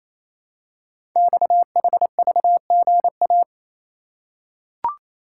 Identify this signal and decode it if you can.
Morse code, heard as a keyed tone, about 27 wpm, 700 Hz. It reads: XHVGA
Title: Code Groups - 5 Letters - 28wpm